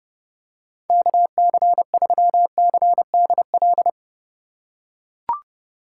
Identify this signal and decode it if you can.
Morse code, heard as a keyed tone, about 32 wpm, 700 Hz. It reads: KC3CDL
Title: USA Call Signs - 6 Characters - 2x Character Spacing - 30wpm